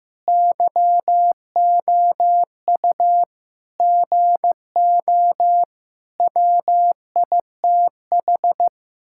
Morse code, heard as a keyed tone, about 15 wpm, 700 Hz. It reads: YOU GO WITH
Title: Sentences from Top 1000 Words - Review Part 6 - 15wpm